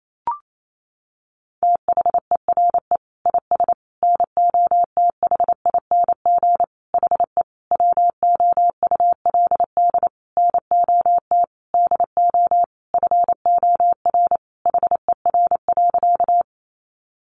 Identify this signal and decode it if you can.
Morse code, heard as a keyed tone, about 28 wpm, 700 Hz. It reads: THERE IS NOTHING HE WOULD NOT DO FOR HER.
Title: Sentences from Top 600 Words 28wpm